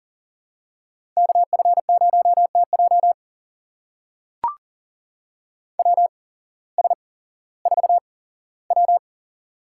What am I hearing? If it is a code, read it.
Morse code, heard as a keyed tone, about 40 wpm, spaced as for 19 wpm, 700 Hz. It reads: KF0TJ WS4W
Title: USA Call Signs - 8x Character Spacing - 40wpm